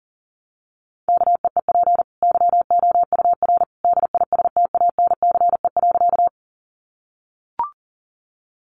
Morse code, heard as a keyed tone, about 40 wpm, 700 Hz. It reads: KEEP YOUR DISTANCE.
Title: Sentences from Top 1200 Words 40wpm